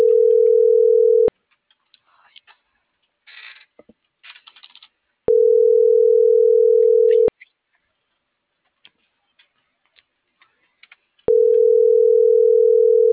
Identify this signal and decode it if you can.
Ringback tone (North America), 440 + 480 Hz.
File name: CENTRAL NMP / out melanie.diaznmp - .43301